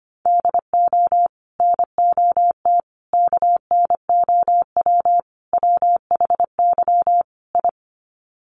Morse code, heard as a keyed tone, about 25 wpm, 700 Hz. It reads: DO NOT KNOW WHY I
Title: Sets of 5 Words Encore 25wpm